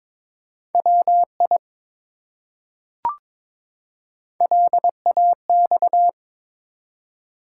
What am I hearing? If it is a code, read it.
Morse code, heard as a keyed tone, about 22 wpm, 700 Hz. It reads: WI LAX
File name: ARRL Sections v2 - 22wpm